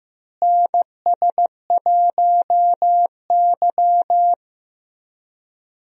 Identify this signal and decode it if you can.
Morse code, heard as a keyed tone, about 15 wpm, 700 Hz. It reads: NS1Y
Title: USA Call Signs - 4 Characters - Repeated 3x - 15wpm